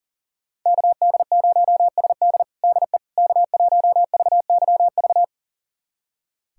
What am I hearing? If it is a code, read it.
Morse code, heard as a keyed tone, about 40 wpm, 700 Hz. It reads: KD0SD DE K1VYV
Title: USA Callsign DE Callsign 40wpm